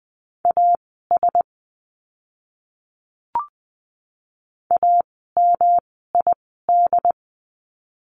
Morse code, heard as a keyed tone, about 20 wpm, spaced as for 15 wpm, 700 Hz. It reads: AS AMID
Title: 108 Lesson 18f - Letter m - Words 20wpm